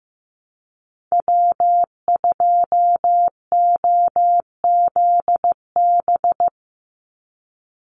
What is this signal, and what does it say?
Morse code, heard as a keyed tone, about 15 wpm, 700 Hz. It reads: W2OZB
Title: Call Signs Repeated 3x - 15wpm